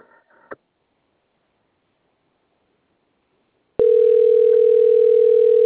Ringback tone (North America), 440 + 480 Hz.
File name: rg-1600-unknown- - - .56364